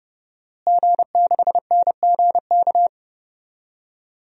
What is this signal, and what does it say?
Morse code, heard as a keyed tone, about 32 wpm, 700 Hz. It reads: G6NGK
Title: DXCC-100 Call Signs - 5 Characters - Repeated 3x - 30wpm